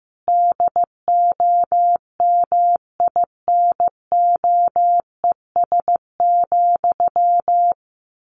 Morse code, heard as a keyed tone, about 15 wpm, 700 Hz. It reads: DOMINOES,
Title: A Sound Of Thunder 15wpm